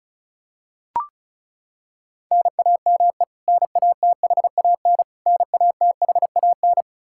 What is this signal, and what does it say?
Morse code, heard as a keyed tone, about 36 wpm, 700 Hz. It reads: NAME NATHAN NATHAN
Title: Practice QSO - Exchange 1 - Beginning - 35wpm